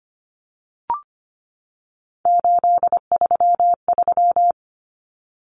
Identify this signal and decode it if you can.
Morse code, heard as a keyed tone, about 25 wpm, 700 Hz. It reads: 833
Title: Common Cut Numbers - 3-Digit 25wpm